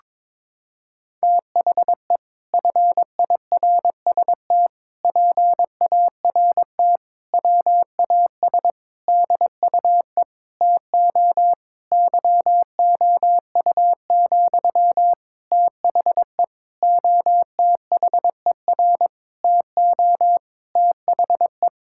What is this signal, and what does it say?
Morse code, heard as a keyed tone, about 22 wpm, 700 Hz. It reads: THE FIRST PART WAS DUE TO YOU, THE OTHER TO THE
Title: Sentences from Top 1000 Words - Review Part 6 - 22wpm